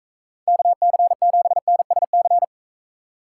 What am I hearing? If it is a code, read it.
Morse code, heard as a keyed tone, about 38 wpm, 700 Hz. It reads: KC7NSC